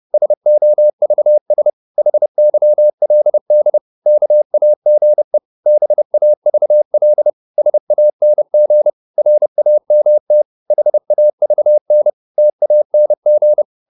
Morse code, heard as a keyed tone, about 30 wpm, 600 Hz. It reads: SOVS HYLD KAGE BAVL SANG RAMT HAVN TANG
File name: Korte ord DK 30wpm